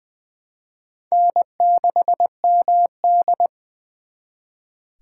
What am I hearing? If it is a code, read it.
Morse code, heard as a keyed tone, about 24 wpm, 700 Hz. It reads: N6MD
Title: Speed Racing - USA 1x2 Call Signs 20wpm